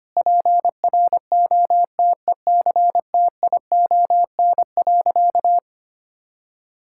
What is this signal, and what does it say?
Morse code, heard as a keyed tone, about 25 wpm, 700 Hz. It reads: PROTECTION.
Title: Sentences from Top 1600 Words 25wpm